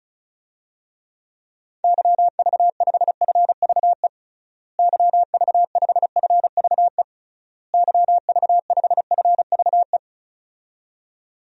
Morse code, heard as a keyed tone, about 36 wpm, 700 Hz. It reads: YV5FVE YV5FVE YV5FVE
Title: DXCC-100 Call Signs - 6 Characters - Repeated 3x - 35wpm